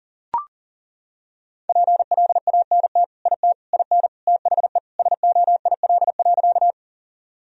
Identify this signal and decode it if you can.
Morse code, heard as a keyed tone, about 40 wpm, 700 Hz. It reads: PLANT IT IN THE SOIL.
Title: Sentences from Top 1000 Words - Review Part 4 - 40wpm